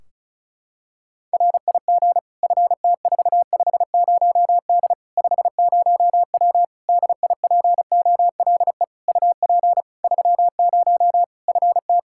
Morse code, heard as a keyed tone, about 36 wpm, 700 Hz. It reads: RIG FT450D 50W DIPOLE UP 30 FT
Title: Practice QSO - Exchange 2 - Info - 35wpm